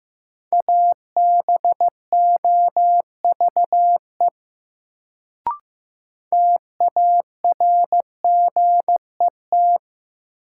Morse code, heard as a keyed tone, about 15 wpm, 700 Hz. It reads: ABOVE TARGET
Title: Sentences from Top 1200 Words - New Word Drill Alt - 15wpm